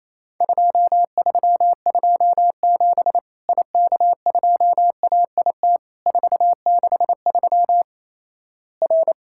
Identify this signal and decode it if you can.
Morse code, heard as a keyed tone, about 27 wpm, 700 Hz. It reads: 2327 IK2AIT 463 R
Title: Worked All Europe DX Contest - QTC Exchange - 28wpm